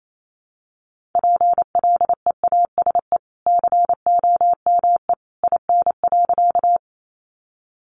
Morse code, heard as a keyed tone, about 28 wpm, 700 Hz. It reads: PLEASE COME IN.
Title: Sentences from Top 1500 Words 28wpm